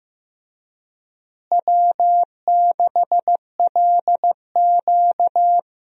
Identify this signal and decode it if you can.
Morse code, heard as a keyed tone, about 15 wpm, 700 Hz. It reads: W6LQ
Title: USA Call Signs - 4 Characters - Repeated 3x - 15wpm